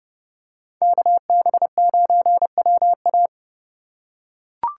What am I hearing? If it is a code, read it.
Morse code, heard as a keyed tone, about 28 wpm, 700 Hz. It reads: KB9WA